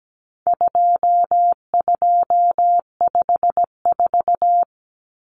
Morse code, heard as a keyed tone, about 17 wpm, 700 Hz. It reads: 2254
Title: Common Cut Numbers - 4-Digit 17wpm